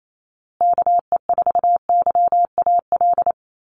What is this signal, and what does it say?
Morse code, heard as a keyed tone, about 28 wpm, 700 Hz. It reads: KE4YAL
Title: USA Call Signs Repeated 3x 28wpm